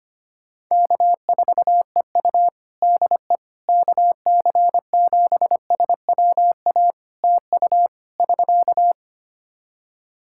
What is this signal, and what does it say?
Morse code, heard as a keyed tone, about 25 wpm, 700 Hz. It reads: K4EU DE KC7SWA TU <SK>
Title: Practice QSO - Exchange 4 - Ending - 25wpm